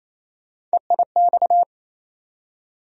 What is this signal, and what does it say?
Morse code, heard as a keyed tone, about 28 wpm, 700 Hz. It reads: EIX
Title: USA Call Sign Postfixes - 3 Letters 28wpm